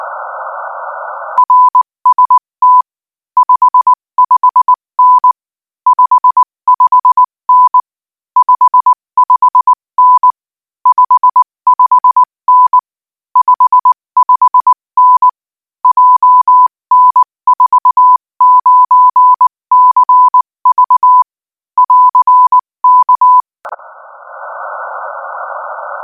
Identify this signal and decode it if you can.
Morse code, heard as a keyed tone, about 19 wpm, 1000 Hz. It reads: RST 55N 55N 55N 55N 55N JN49CV <AR> K
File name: IZ4BEH-13-cm-AS